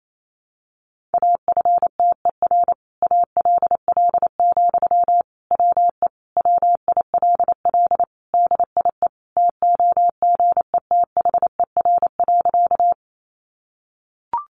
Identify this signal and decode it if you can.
Morse code, heard as a keyed tone, about 28 wpm, 700 Hz. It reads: AFTER ALL, WE WILL DIE TOGETHER.
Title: Sentences from Top 1300 Words 28wpm